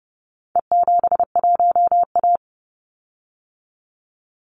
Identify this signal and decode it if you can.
Morse code, heard as a keyed tone, about 30 wpm, 700 Hz. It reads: E71A
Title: Call Signs - 30wpm